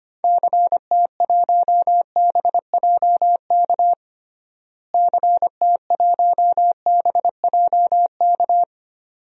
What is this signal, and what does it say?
Morse code, heard as a keyed tone, about 25 wpm, 700 Hz. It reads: CT1BJK CT1BJK
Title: DXCC-100 Call Signs - 6 Characters - Repeated 2x - 25wpm